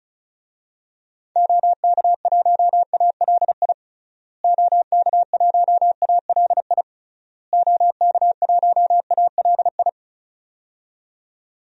Morse code, heard as a keyed tone, about 36 wpm, 700 Hz. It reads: OK1ALI OK1ALI OK1ALI
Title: DXCC-100 Call Signs - 6 Characters - Repeated 3x - 35wpm